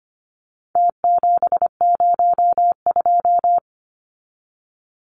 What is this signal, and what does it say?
Morse code, heard as a keyed tone, about 25 wpm, 700 Hz. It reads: T702
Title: Numbers - Contest Style - 4-Digit 25wpm